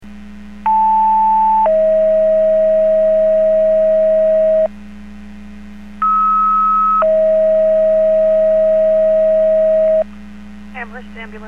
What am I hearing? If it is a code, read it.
Two-tone page, heard as a one-second tone, then a three-second tone, about 880 then 640 Hz, about 1290 then 640 Hz. It is heard from 0.7 to 10.0 s.